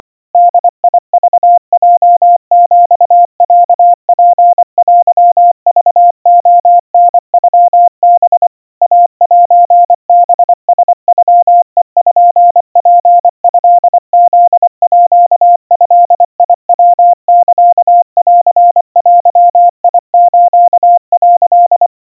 Morse code (Japanese Wabun code), heard as a keyed tone, about 24 wpm, 700 Hz. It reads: ボクヲヒロツテクレタノハ イセハラノペツトフードヤサンデス」
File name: 90 cpm 700 Hz Crispy